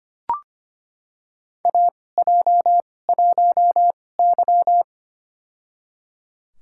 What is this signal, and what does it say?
Morse code, heard as a keyed tone, about 25 wpm, spaced as for 18 wpm, 700 Hz. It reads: AJ1Y
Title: USA Call Signs - 4 Characters - 2x Character Spacing - 25wpm